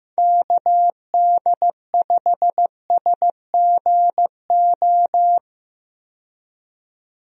Morse code, heard as a keyed tone, about 15 wpm, 700 Hz. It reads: KD5SGO
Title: USA Call Signs - 6 Characters - 15wpm